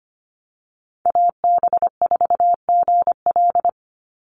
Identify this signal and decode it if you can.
Morse code, heard as a keyed tone, about 25 wpm, 700 Hz. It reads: AB4GL
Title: USA Call Signs 25wpm